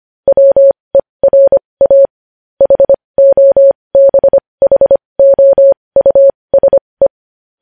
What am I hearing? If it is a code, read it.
Morse code, heard as a keyed tone, about 25 wpm, spaced as for 20 wpm, 550 Hz. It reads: WERA HOBHOUSE
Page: News Headlines in Morse Code at 20 WPM